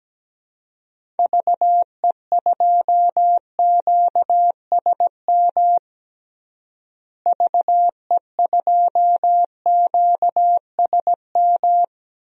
Morse code, heard as a keyed tone, about 17 wpm, 700 Hz. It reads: VE2QSM VE2QSM
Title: DXCC-100 Call Signs - 6 Characters - Repeated 2x - 17wpm